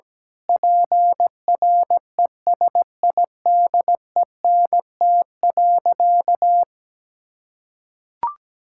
Morse code, heard as a keyed tone, about 17 wpm, 700 Hz. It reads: PRESIDENT.
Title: Sentences from Top 1600 Words 17wpm